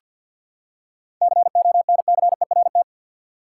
Morse code, heard as a keyed tone, about 47 wpm, 700 Hz. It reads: CONCERT
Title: Seven Letter Words - 50wpm